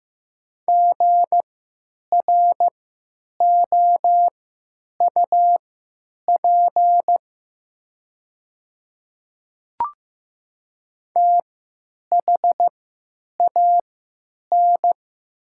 Morse code, heard as a keyed tone, about 15 wpm, spaced as for 9 wpm, 700 Hz. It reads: GROUP THAN
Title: 3x Character Spacing - Top 500 Words - 15wpm